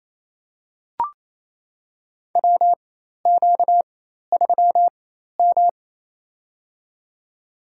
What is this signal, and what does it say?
Morse code, heard as a keyed tone, about 27 wpm, spaced as for 13 wpm, 700 Hz. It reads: WQ3M
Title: USA Call Signs - 4 Characters - 4x Character Spacing - 28wpm